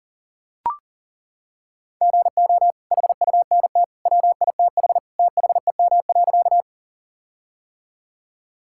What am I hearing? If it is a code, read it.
Morse code, heard as a keyed tone, about 40 wpm, 700 Hz. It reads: GO HUNT WITH THEM.
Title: Sentences from Top 1600 Words 40wpm